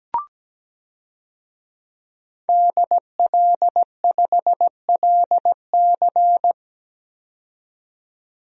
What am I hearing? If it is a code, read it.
Morse code, heard as a keyed tone, about 17 wpm, 700 Hz. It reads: DL5LC
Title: DXCC-100 Call Signs - 5 Characters - 17wpm